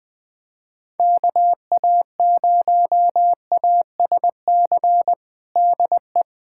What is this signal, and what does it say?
Morse code, heard as a keyed tone, about 20 wpm, 700 Hz. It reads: KA0ASC DE
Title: USA Callsign DE Callsign - 6 Characters - 20wpm